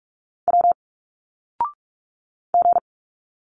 Morse code, heard as a keyed tone, about 44 wpm, 700 Hz. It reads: W G
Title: Single Letters 45wpm